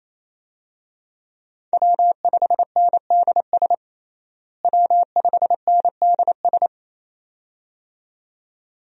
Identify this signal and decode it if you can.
Morse code, heard as a keyed tone, about 28 wpm, 700 Hz. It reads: W5NDS W5NDS
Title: USA Call Signs - 5 Characters - Repeated 2x - 28wpm